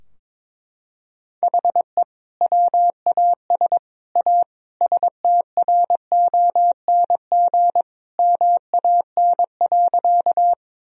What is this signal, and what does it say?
Morse code, heard as a keyed tone, about 22 wpm, 700 Hz. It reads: HE WAS A STRONG MAN.
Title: Sentences from Top 1000 Words - Review Part 3 - 22wpm